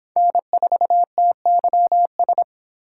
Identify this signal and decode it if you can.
Morse code, heard as a keyed tone, about 24 wpm, 700 Hz. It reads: N4TYS